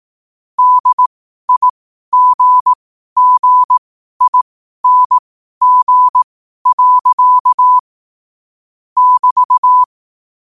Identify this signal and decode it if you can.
Morse code, heard as a keyed tone, about 18 wpm, spaced as for 12 wpm, 1000 Hz. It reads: DIGGING. <BT>
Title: 12 WPM morse code quotes for Thu, 29 Feb 2024